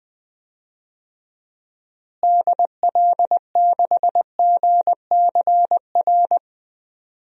Morse code, heard as a keyed tone, about 21 wpm, 700 Hz. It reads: DL6GCR
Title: DXCC-100 Call Signs - 6 Characters - Repeated 3x - 20wpm